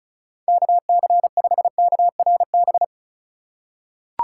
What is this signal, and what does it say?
Morse code, heard as a keyed tone, about 35 wpm, 700 Hz. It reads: KC5KRB
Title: USA Call Signs - 6 Characters - 3x Character Spacing - 35wpm